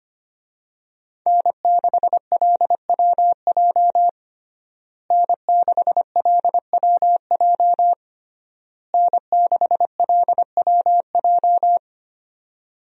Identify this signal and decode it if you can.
Morse code, heard as a keyed tone, about 25 wpm, 700 Hz. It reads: N6LWJ N6LWJ N6LWJ
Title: USA Call Signs - 5 Characters - Repeated 3x - 25wpm